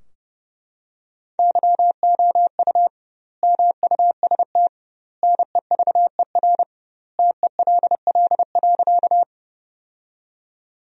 Morse code, heard as a keyed tone, about 30 wpm, 700 Hz. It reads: YOU MUST NEVER TELL.
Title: 2x Word-Spacing - Sentences from Top 300 Words - 30wpm